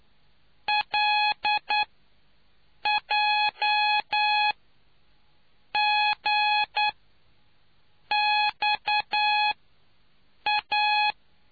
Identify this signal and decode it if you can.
Morse code, heard as a keyed tone, about 9 wpm, spaced as for 5 wpm, 840 Hz. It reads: LJGXA